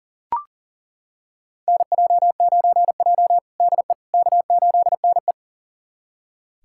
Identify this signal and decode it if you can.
Morse code, heard as a keyed tone, about 40 wpm, 700 Hz. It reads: NJ9J DE K8NE
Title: USA Callsign DE Callsign - 4 Characters - 40wpm